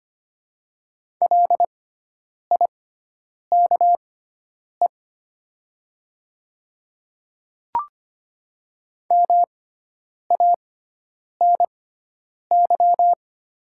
Morse code, heard as a keyed tone, about 25 wpm, spaced as for 9 wpm, 700 Hz. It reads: LIKE MANY
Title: 6x Character Spacing - Top 100 Words - 25wpm